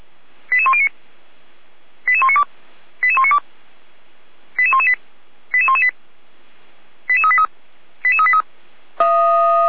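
Five-tone selective call (ZVEI), heard as quick runs of five tones, tones of about 70 ms. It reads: 88109 88192 88192 88118 88118 88383 88383
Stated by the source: Five-tone ZVEI selective call received by SDR, from the fire brigade (Feuerwehr) Orlamünde recording server.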